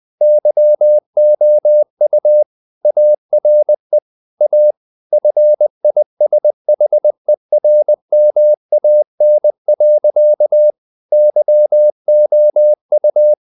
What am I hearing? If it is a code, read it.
Morse code, heard as a keyed tone, about 20 wpm, 600 Hz. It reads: YOU ARE A FISHERMAN. YOU
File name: the old man and the sea part 2 20wpm